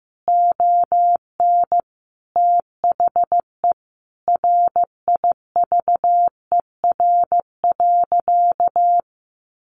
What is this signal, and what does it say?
Morse code, heard as a keyed tone, about 15 wpm, 700 Hz. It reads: ON THE RIVER.
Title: Sentences from Top 1300 Words 15wpm